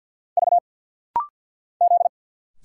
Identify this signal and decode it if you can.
Morse code, heard as a keyed tone, about 49 wpm, 700 Hz. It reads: V Z
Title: Single Letter-Number Alt - Rapid-Fire 50wpm